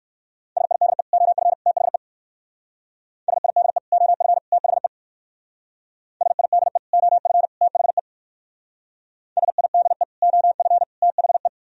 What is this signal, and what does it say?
Morse code, heard as a keyed tone, about 58 wpm, 700 Hz. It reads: SIDE OF THE SIDE OF THE SIDE OF THE SIDE OF THE